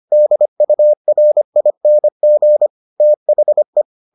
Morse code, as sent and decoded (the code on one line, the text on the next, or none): -.. ..- .-. .. -. --. / - .... .
DURING THE